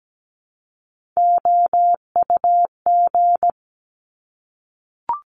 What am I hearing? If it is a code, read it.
Morse code, heard as a keyed tone, about 17 wpm, 700 Hz. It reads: OUG
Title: Top Three Letter Combinations in Words 17wpm